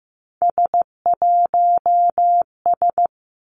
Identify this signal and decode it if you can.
Morse code, heard as a keyed tone, about 15 wpm, 700 Hz. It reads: S1S
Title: Three Challenging Letter-Number Combinations 15wpm